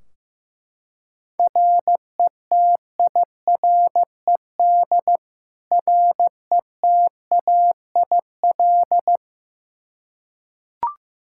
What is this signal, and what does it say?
Morse code, heard as a keyed tone, about 16 wpm, 700 Hz. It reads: RETIRED RETAIL
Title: K1USN SST QSO Practice 15wpm